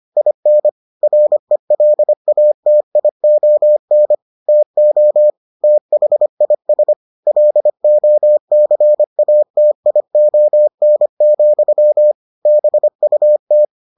Morse code, heard as a keyed tone, about 25 wpm, 600 Hz. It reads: IN RELATION TO THIS LOCATION, BUT